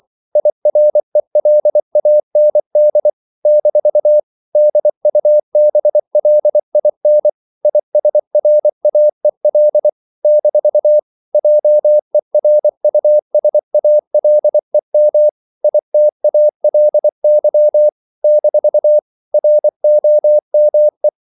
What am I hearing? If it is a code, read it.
Morse code, heard as a keyed tone, about 24 wpm, 600 Hz. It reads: IRELAND - DUBLIN ISRAEL - JERUSALEM ITALY - ROME
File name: CW / Practice / NationalCapitals24